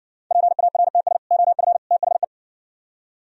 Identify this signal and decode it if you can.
Morse code, heard as a keyed tone, about 60 wpm, 700 Hz. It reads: PARTS OF THE